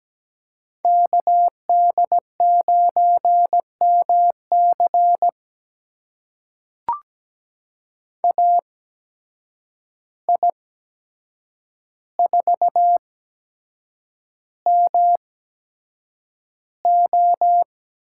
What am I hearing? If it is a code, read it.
Morse code, heard as a keyed tone, about 17 wpm, spaced as for 7 wpm, 700 Hz. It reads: KD9MC AI4MO
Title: USA Call Signs - 5 Characters - 8x Character Spacing - 17wpm